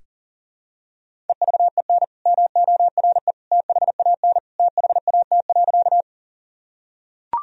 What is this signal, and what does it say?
Morse code, heard as a keyed tone, about 40 wpm, 700 Hz. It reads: EVEN MORE THAN THAT.
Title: Sentences from Top 1000 Words - Review Part 1 - 40wpm